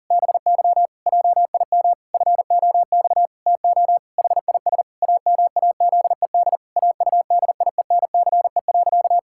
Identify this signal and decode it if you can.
Morse code, heard as a keyed tone, about 40 wpm, 700 Hz. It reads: BY JIM FOX TO HIS AMAZED AUDIENCE.
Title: Pangram Challenge v2 - 40wpm